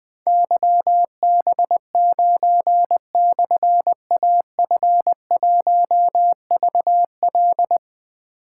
Morse code, heard as a keyed tone, about 20 wpm, 700 Hz. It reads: YB9/AF1VL
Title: Difficult Call Signs 20wpm